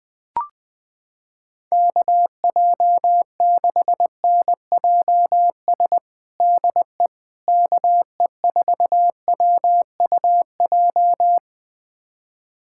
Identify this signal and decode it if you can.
Morse code, heard as a keyed tone, about 20 wpm, 700 Hz. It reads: KJ6NJS DE KE4WUJ
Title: USA Callsign DE Callsign 20wpm